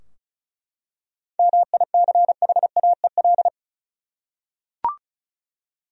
Morse code, heard as a keyed tone, about 36 wpm, 700 Hz. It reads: MICHAEL
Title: USA Common Names - 35wpm